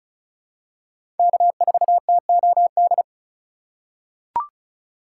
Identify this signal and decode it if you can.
Morse code, heard as a keyed tone, about 36 wpm, 700 Hz. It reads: K4TOD
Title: USA Call Signs - 5 Characters - 2x Character Spacing - 35wpm